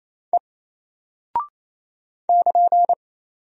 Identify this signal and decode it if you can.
Morse code, heard as a keyed tone, about 27 wpm, 700 Hz. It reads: E <KN>
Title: Single Character (Letter:Number:Punctuation:Prosign) Alt 28wpm